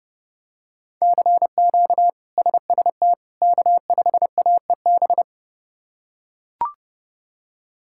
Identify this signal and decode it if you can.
Morse code, heard as a keyed tone, about 30 wpm, 700 Hz. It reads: CQ SST K5AEB
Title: K1USN SST QSO Practice - 30wpm